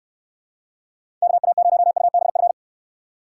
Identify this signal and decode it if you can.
Morse code, heard as a keyed tone, about 69 wpm, 700 Hz. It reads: KA0UBF